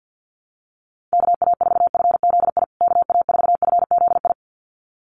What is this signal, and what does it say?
Morse code, heard as a keyed tone, about 49 wpm, 700 Hz. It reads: KA4FZI KA4FZI
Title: USA Call Signs Repeated 2x 50wpm